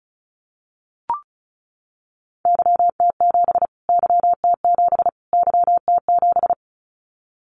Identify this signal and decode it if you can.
Morse code, heard as a keyed tone, about 34 wpm, 700 Hz. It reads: YT7 YT7 YT7
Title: Callsign Prefixes for Top 25 DXCC - Repeated 3x - 35wpm